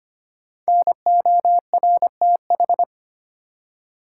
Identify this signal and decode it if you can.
Morse code, heard as a keyed tone, about 25 wpm, 700 Hz. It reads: NORTH
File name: 5 Letter Words - First 3 Letters Given - 25wpm